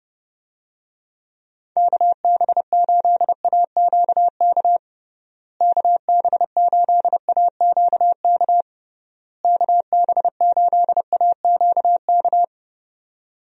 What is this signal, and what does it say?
Morse code, heard as a keyed tone, about 30 wpm, 700 Hz. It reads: KB8AQK KB8AQK KB8AQK
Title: USA Call Signs - 6 Characters - Repeated 3x - 30wpm